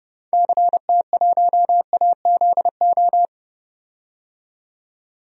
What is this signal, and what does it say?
Morse code, heard as a keyed tone, about 31 wpm, 700 Hz. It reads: CT1AZO
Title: DXCC-100 Call Signs - 6 Characters - 30wpm